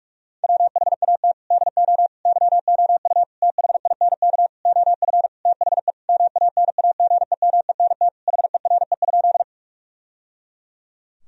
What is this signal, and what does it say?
Morse code, heard as a keyed tone, about 46 wpm, 700 Hz. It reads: WHAT DO YOU THINK OF THE MANAGEMENT HERE?
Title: Sentences from Top 1000 Words - Review Part 4 - 45wpm